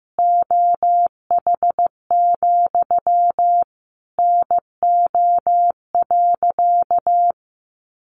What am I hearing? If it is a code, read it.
Morse code, heard as a keyed tone, about 15 wpm, 700 Hz. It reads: OH, NO.
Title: Sentences from Top 1200 Words 15wpm